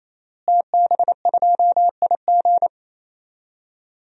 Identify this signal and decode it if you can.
Morse code, heard as a keyed tone, about 27 wpm, 700 Hz. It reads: TB2IG
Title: DXCC-100 Call Signs - 5 Characters - Repeated 3x - 28wpm